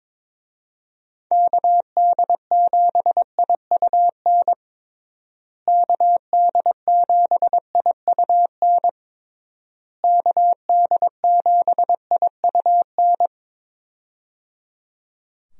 Morse code, heard as a keyed tone, about 21 wpm, 700 Hz. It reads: KD7IUN KD7IUN KD7IUN
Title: USA Call Signs - 6 Characters - Repeated 3x - 22wpm